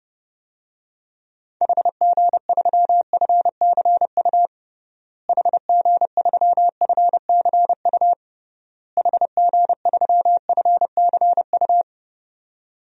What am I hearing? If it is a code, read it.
Morse code, heard as a keyed tone, about 30 wpm, 700 Hz. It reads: HG3FCU HG3FCU HG3FCU
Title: DXCC-100 Call Signs - 6 Characters - Repeated 3x - 30wpm